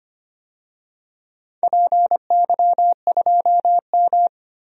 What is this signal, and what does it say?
Morse code, heard as a keyed tone, about 25 wpm, 700 Hz. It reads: PY2M
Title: DXCC-100 Call Signs - 4 Characters - 25wpm